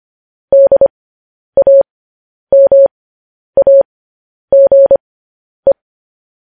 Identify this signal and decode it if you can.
Morse code, heard as a keyed tone, about 25 wpm, spaced as for 10 wpm, 550 Hz. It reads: DAMAGE
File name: podcast / News headlines-1x-25x10